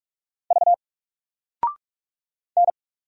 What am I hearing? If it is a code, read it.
Morse code, heard as a keyed tone, about 44 wpm, 700 Hz. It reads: V N
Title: Single Letter-Number 45wpm